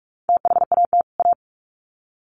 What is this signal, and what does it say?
Morse code, heard as a keyed tone, about 44 wpm, 700 Hz. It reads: THAT A
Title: Sets of 2 Words 45wpm